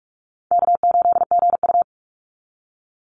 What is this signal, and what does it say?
Morse code, heard as a keyed tone, about 44 wpm, 700 Hz. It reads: K8GU
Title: Easy Call Signs Repeated 3x - 45wpm